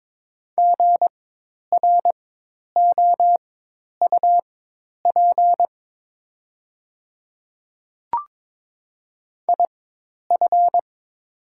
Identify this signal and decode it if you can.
Morse code, heard as a keyed tone, about 22 wpm, spaced as for 10 wpm, 700 Hz. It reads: GROUP IF
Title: 4x Character Spacing - Top 300 Words - 22wpm